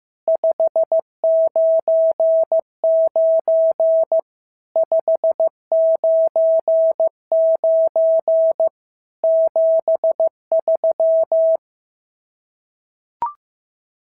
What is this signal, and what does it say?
Morse code, heard as a keyed tone, about 15 wpm, 650 Hz. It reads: 599 599 73
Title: SOTA QSO Practice 15wpm